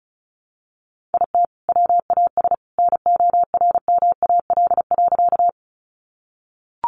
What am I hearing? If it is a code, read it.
Morse code, heard as a keyed tone, about 34 wpm, 700 Hz. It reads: IT WAS NORMAL.
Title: Sentences from Top 1200 Words 35wpm